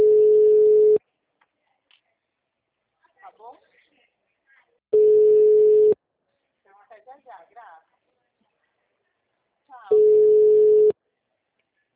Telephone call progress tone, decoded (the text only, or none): ringback tone